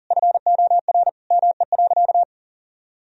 Morse code, heard as a keyed tone, about 40 wpm, 700 Hz. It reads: FOR ME.
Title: Sentences from Top 1000 Words - Review Part 4 - 40wpm